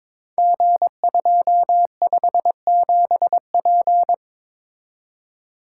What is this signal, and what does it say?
Morse code, heard as a keyed tone, about 21 wpm, 700 Hz. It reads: G257P
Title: Code Groups - 5 Letter:Numbers - 22wpm